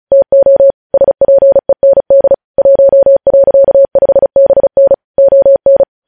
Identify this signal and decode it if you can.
Morse code, heard as a keyed tone, about 36 wpm, 550 Hz. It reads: TO SPEND 1.5BN ON
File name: podcast / News headlines-1x-35x35